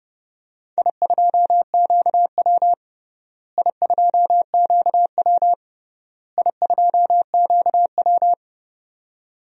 Morse code, heard as a keyed tone, about 30 wpm, 700 Hz. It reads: I2QW I2QW I2QW
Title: DXCC-100 Call Signs - 4 Characters - Repeated 3x - 30wpm